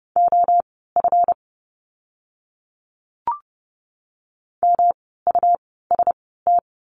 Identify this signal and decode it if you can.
Morse code, heard as a keyed tone, about 30 wpm, spaced as for 17 wpm, 700 Hz. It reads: OF MUST
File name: course / 30:10wpm / 237 Lesson 38f - Letter q - Words 30wpm